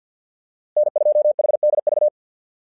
Morse code, heard as a keyed tone, about 50 wpm, 600 Hz. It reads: N2HDV
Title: SOTA QSO Practice 50wpm